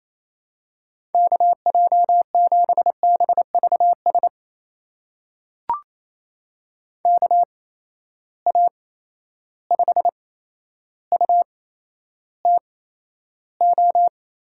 Morse code, heard as a keyed tone, about 28 wpm, spaced as for 12 wpm, 700 Hz. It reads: KJ7BVS KA5UTO
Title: USA Call Signs - 6 Characters - 8x Character Spacing - 28wpm